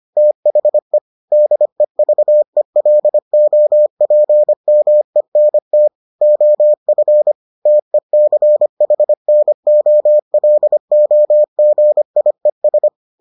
Morse code, heard as a keyed tone, about 25 wpm, 600 Hz. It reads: THE DEVELOPMENT OF TECHNOLOGIES